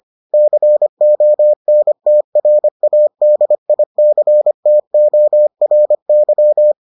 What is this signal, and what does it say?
Morse code, heard as a keyed tone, about 25 wpm, 600 Hz. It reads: CONTRADICTORY